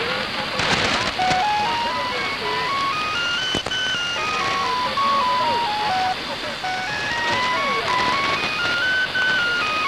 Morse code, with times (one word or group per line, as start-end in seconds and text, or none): none